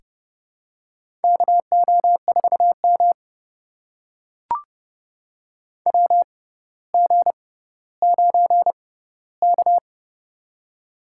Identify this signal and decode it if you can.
Morse code, heard as a keyed tone, about 30 wpm, spaced as for 15 wpm, 700 Hz. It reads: KO4M WG9K
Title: USA Call Signs - 6x Character Spacing - 30wpm